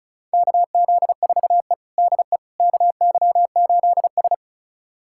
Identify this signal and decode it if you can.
Morse code, heard as a keyed tone, about 36 wpm, 700 Hz. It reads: KZ4E DE KY8S
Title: USA Callsign DE Callsign - 4 Characters - 35wpm